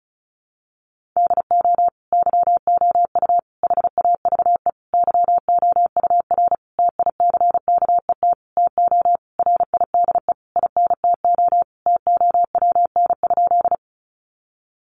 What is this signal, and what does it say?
Morse code, heard as a keyed tone, about 34 wpm, 700 Hz. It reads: DO YOU HAVE YOUR TICKET TO RIDE INTO TOWN?
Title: Sentences from Top 1400 Words 35wpm